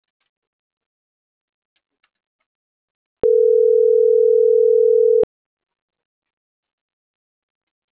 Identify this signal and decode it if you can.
Ringback tone (North America), 440 + 480 Hz.